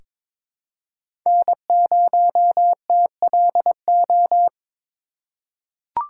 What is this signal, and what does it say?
Morse code, heard as a keyed tone, about 22 wpm, 700 Hz. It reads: N0TLO
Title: USA Call Signs - 5 Characters - 4x Character Spacing - 22wpm